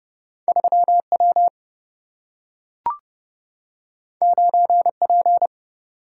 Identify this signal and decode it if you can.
Morse code, heard as a keyed tone, about 30 wpm, 700 Hz. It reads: 3W 9P
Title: Single-Letter Single-Number 30wpm